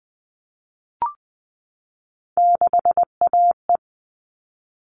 Morse code, heard as a keyed tone, about 20 wpm, 700 Hz. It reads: 6AE
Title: All Cut Numbers - 3-Digit 20wpm